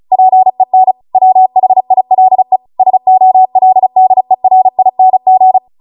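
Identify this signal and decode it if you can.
Morse code, heard as a keyed tone, about 36 wpm, 750 Hz. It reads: PEN WHILE SOLDERING
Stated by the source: Amateur radio Morse code practice archive transmission at 35 wpm.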